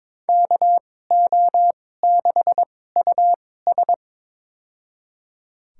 Morse code, heard as a keyed tone, about 21 wpm, spaced as for 16 wpm, 700 Hz. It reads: KO6US
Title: USA Call Signs - 2x Character Spacing - 22wpm